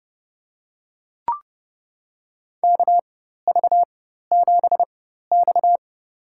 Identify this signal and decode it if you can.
Morse code, heard as a keyed tone, about 31 wpm, spaced as for 14 wpm, 690 Hz. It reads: KV7X
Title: USA Call Signs - 4x Character Spacing - 30wpm